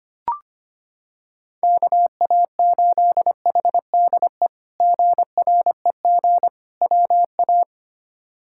Morse code, heard as a keyed tone, about 25 wpm, 700 Hz. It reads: KA8HDE GREG WA
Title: K1USN SST QSO Practice - 25wpm